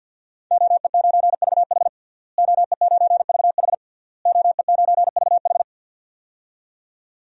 Morse code, heard as a keyed tone, about 53 wpm, 700 Hz. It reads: OE9VH OE9VH OE9VH
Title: DXCC-100 Call Signs - 5 Characters - Repeated 3x - 50wpm